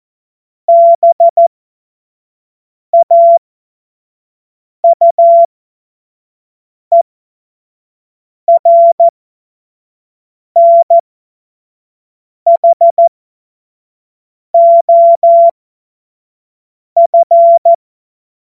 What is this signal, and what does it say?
Morse code, heard as a keyed tone, about 14 wpm, spaced as for 5 wpm, 680 Hz. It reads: BAUERNHOF
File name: Morsekurs / Witze / Witz 30. August 2022 25 F70